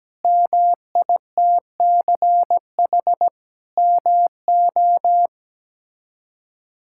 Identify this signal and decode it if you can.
Morse code, heard as a keyed tone, about 17 wpm, 700 Hz. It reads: MITCH MO
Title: K1USN SST QSO Practice - 17wpm